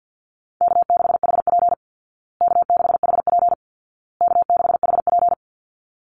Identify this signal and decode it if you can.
Morse code, heard as a keyed tone, about 50 wpm, 700 Hz. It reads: K6HP K6HP K6HP
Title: Easy Call Signs Repeated 3x - 50wpm